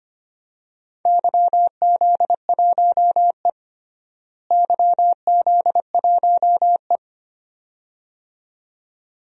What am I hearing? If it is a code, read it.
Morse code, heard as a keyed tone, about 25 wpm, 700 Hz. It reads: YZ1E YZ1E
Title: DXCC-100 Call Signs - 4 Characters - Repeated 2x - 25wpm